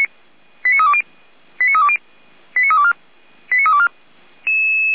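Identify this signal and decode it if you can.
Five-tone selective call (ZVEI), heard as quick runs of five tones, tones of about 70 ms. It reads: 89320 89320 89324 89324
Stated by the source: ZVEI five-tone call received by SDR, station feuerwehr-orlamuende.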